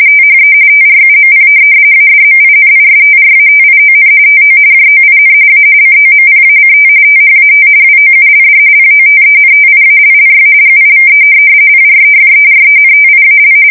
RTTY, about 45.45 baud. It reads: THE TRUMP ADMINISTRATION TO JOIN A COMPACT THAT WOULD POTENTIALLY GIVE PREFERENTIAL
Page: 170 Shift 2125/2295